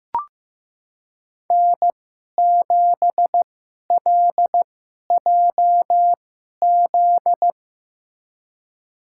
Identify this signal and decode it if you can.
Morse code, heard as a keyed tone, about 15 wpm, spaced as for 11 wpm, 700 Hz. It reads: N7LJZ
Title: USA Call Signs - 2x Character Spacing - 15wpm